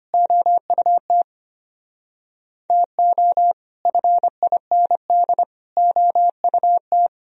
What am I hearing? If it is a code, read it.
Morse code, heard as a keyed tone, about 30 wpm, 700 Hz. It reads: OUT TO FIND OUT
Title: Speed-Racing - Sets of 3 Words 25wpm